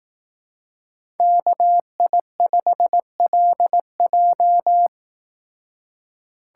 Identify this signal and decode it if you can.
Morse code, heard as a keyed tone, about 18 wpm, 700 Hz. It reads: KI5LJ
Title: Speed Racing - USA 2x2 Call Signs 15wpm